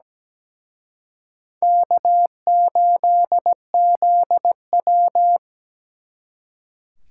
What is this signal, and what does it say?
Morse code, heard as a keyed tone, about 20 wpm, 700 Hz. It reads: K8ZW
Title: Speed Racing - USA 1x2 Call Signs 17wpm